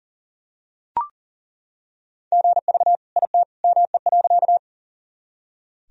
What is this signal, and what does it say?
Morse code, heard as a keyed tone, about 40 wpm, 700 Hz. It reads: GV IT ME.
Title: Sentences from Top 200 Words and Abbreviations 40wpm